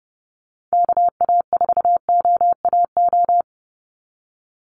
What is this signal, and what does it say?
Morse code, heard as a keyed tone, about 30 wpm, 700 Hz. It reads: KA4OAO
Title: USA Call Signs 30wpm